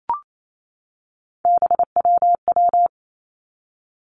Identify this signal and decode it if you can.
Morse code, heard as a keyed tone, about 28 wpm, 700 Hz. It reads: BWW
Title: Three Challenging Letter-Number Combinations 28wpm